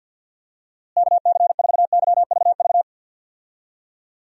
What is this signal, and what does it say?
Morse code, heard as a keyed tone, about 51 wpm, 700 Hz. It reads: KC4YVV
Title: USA Call Signs - 6 Characters - Repeated 3x - 50wpm